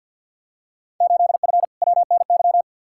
Morse code, heard as a keyed tone, about 52 wpm, 700 Hz. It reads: 8F WNY
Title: Field Day Exchanges Encore - 50wpm